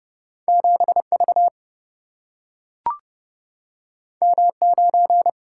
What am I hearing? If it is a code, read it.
Morse code, heard as a keyed tone, about 31 wpm, 700 Hz. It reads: 7V M9
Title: Single-Letter Single-Number 30wpm